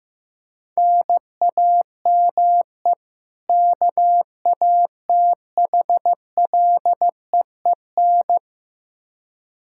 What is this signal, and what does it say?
Morse code, heard as a keyed tone, about 15 wpm, 700 Hz. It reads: NAME KATHLEEN
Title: Practice QSO - Exchange 1 - Beginning - 15wpm